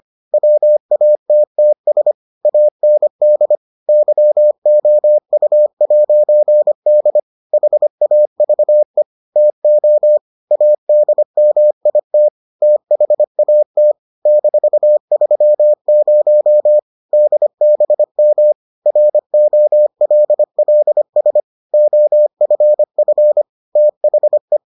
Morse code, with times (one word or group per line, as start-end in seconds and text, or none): 0.3-2.1 WATTS
2.5-3.6 AND
3.9-7.2 YOU'D
7.5-9.0 HAVE
9.4-10.2 TO
10.5-12.3 ADMIT
12.6-13.9 THAT
14.3-16.8 -30
17.1-18.5 DBM
18.9-21.4 ROLLS
21.7-23.4 OFF
23.8-24.6 THE